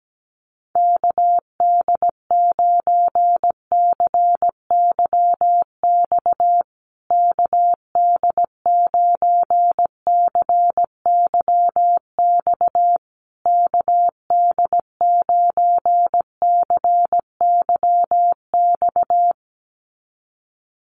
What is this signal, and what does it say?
Morse code, heard as a keyed tone, about 17 wpm, 700 Hz. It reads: KD9CYX KD9CYX KD9CYX
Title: USA Call Signs Repeated 3x 17wpm